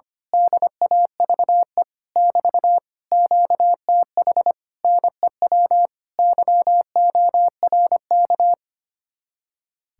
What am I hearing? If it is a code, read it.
Morse code, heard as a keyed tone, about 24 wpm, 700 Hz. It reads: DAVE <BT> QTH NEW YORK
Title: Speed-Racing Common QSO Phrases Encore - 25wpm